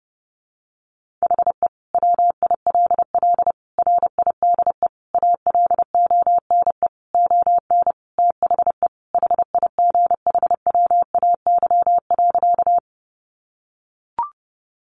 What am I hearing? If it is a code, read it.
Morse code, heard as a keyed tone, about 30 wpm, 700 Hz. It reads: HE WILL RIDE ALONE ON THE HIGHWAY.
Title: Sentences from Top 1400 Words 30wpm